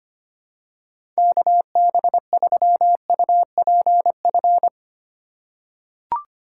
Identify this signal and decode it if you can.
Morse code, heard as a keyed tone, about 25 wpm, 700 Hz. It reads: KB3UPF
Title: USA Call Signs - 6 Characters - 4x Character Spacing - 25wpm